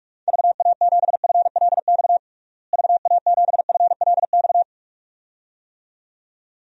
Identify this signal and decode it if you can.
Morse code, heard as a keyed tone, about 46 wpm, 700 Hz. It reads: VA7FLX VA7FLX
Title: DXCC-100 Call Signs - 6 Characters - Repeated 2x - 45wpm